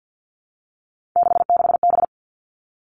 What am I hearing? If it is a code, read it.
Morse code, heard as a keyed tone, about 50 wpm, 700 Hz. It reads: 66B
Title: All Cut Numbers - 3-Digit 50wpm